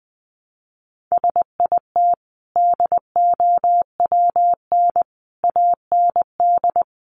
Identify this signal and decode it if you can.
Morse code, heard as a keyed tone, about 20 wpm, 700 Hz. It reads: SIT DOWN AND
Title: Sentences from Top 1300 Words 20wpm